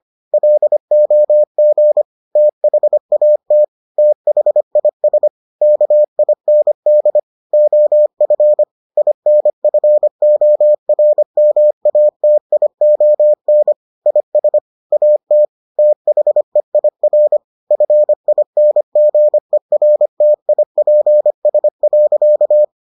Morse code, heard as a keyed tone, about 25 wpm, 600 Hz. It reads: LOG THAT THIS KIND OF INFORMATION IS AT THEIR FINGERTIPS.